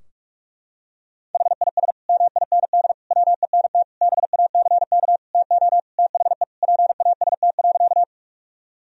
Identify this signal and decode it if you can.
Morse code, heard as a keyed tone, about 41 wpm, 700 Hz. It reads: HIS MIND WENT BACK TO THE PAST.
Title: Sentences from Top 1000 Words - Review Part 3 - 45wpm